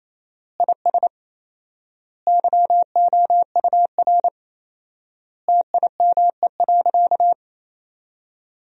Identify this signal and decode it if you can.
Morse code, heard as a keyed tone, about 28 wpm, 700 Hz. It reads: IS YOUR TIME.
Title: 4x Word-Spacing - Sentences from Top 100 Words - 28wpm